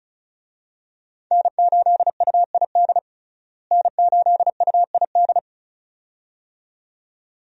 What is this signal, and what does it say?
Morse code, heard as a keyed tone, about 34 wpm, 700 Hz. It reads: N8UID N8UID
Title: USA Call Signs - 5 Characters - Repeated 2x - 35wpm